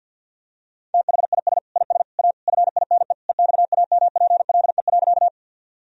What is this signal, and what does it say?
Morse code, heard as a keyed tone, about 52 wpm, 700 Hz. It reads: THIS IS A FINE EXAMPLE.
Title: Sentences from Top 1000 Words - Review Part 1 - 50wpm